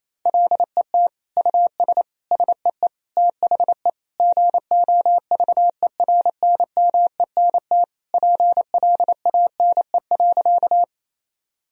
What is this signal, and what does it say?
Morse code, heard as a keyed tone, about 28 wpm, 700 Hz. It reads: LET US SEE THE GOVERNMENT PLANE.
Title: Sentences from Top 1000 Words - Review Part 3 - 28wpm